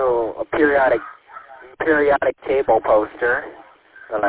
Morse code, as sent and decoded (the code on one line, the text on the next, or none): none